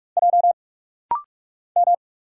Morse code, heard as a keyed tone, about 44 wpm, 700 Hz. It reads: J M
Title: Single Letter-Number Alt - Rapid-Fire 45wpm